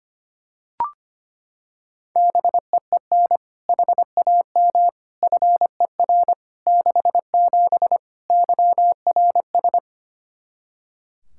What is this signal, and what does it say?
Morse code, heard as a keyed tone, about 25 wpm, 700 Hz. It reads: BEEN HAM FER 67 YRS
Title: Practice QSO - Exchange 3 - More Info - 25wpm